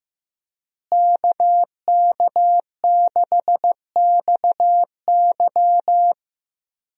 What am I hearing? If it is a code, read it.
Morse code, heard as a keyed tone, about 14 wpm, 700 Hz. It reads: KK6XY
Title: USA Call Signs - 5 Characters - Repeated 2x - 15wpm